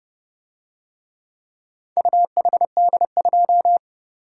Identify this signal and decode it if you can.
Morse code, heard as a keyed tone, about 29 wpm, 700 Hz. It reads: UHD2